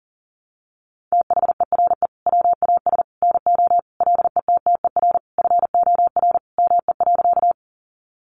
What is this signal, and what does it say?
Morse code, heard as a keyed tone, about 40 wpm, 700 Hz. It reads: THERE WAS NO LETTER FOR ME.
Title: Sentences from Top 300 Words 40wpm